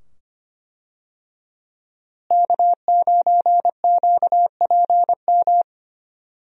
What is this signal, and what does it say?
Morse code, heard as a keyed tone, about 25 wpm, 700 Hz. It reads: K9QPM
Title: USA Call Signs - 5 Characters - Repeated 2x - 25wpm